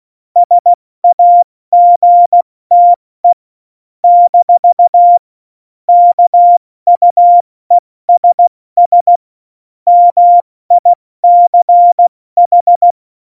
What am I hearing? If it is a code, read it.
Morse code, heard as a keyed tone, about 16 wpm, 700 Hz. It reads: SAGTE - KUESS MICH
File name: Morsekurs / Witze / Witz 29. Dezember 2020, 70 F80